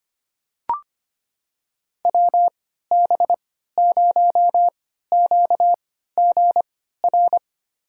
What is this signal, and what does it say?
Morse code, heard as a keyed tone, about 25 wpm, spaced as for 14 wpm, 700 Hz. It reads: WB0QGR
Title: USA Call Signs - 3x Character Spacing - 25wpm